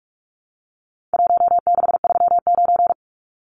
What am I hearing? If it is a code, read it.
Morse code, heard as a keyed tone, about 44 wpm, 700 Hz. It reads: 1639
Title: Numbers - Contest Style - 4-Digit 45wpm